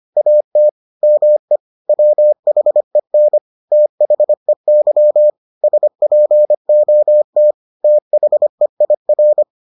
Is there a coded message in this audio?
Yes: Morse code from 0.2 to 9.4 s, heard as a keyed tone, about 25 wpm, 600 Hz. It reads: AT ME WHEN THEY SPOT THEIR